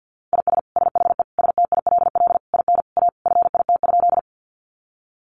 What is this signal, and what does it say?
Morse code, heard as a keyed tone, about 50 wpm, 700 Hz. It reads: IS SHE STILL IN A FIT?
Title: Sentences from Top 800 Words 50wpm